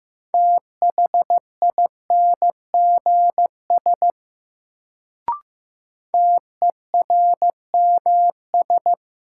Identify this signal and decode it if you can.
Morse code, heard as a keyed tone, about 15 wpm, 700 Hz. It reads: THINGS TERMS
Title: Sets of 3 Words - New Word Drill - 15wpm